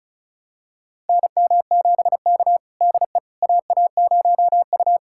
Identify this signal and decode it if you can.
Morse code, heard as a keyed tone, about 34 wpm, 700 Hz. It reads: NM7K DE AA0U
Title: USA Callsign DE Callsign - 4 Characters - 35wpm